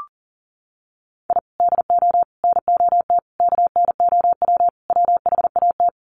Morse code, heard as a keyed tone, about 40 wpm, 700 Hz. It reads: I DO NOT KNOW WHAT
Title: Sets of 5 Words 40wpm